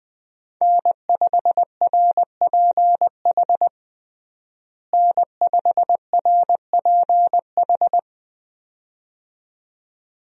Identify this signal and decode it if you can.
Morse code, heard as a keyed tone, about 20 wpm, 700 Hz. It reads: N5RPH N5RPH
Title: USA Call Signs - 5 Characters - Repeated 2x - 20wpm